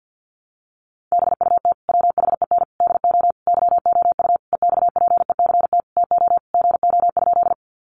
Morse code, heard as a keyed tone, about 50 wpm, 700 Hz. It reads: BUT WHEN DO YOU EXPECT TO GO?
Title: Sentences from Top 1100 Words 50wpm